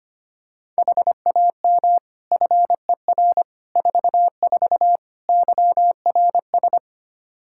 Morse code, heard as a keyed tone, about 25 wpm, 700 Hz. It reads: HAM FER 44 YRS
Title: Practice QSO - Exchange 3 - More Info - 25wpm